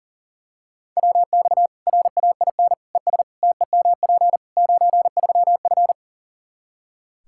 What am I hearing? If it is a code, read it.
Morse code, heard as a keyed tone, about 40 wpm, 700 Hz. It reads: WX RAIN ES TEMP 93F
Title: Practice QSO - Exchange 2 - Info - 40wpm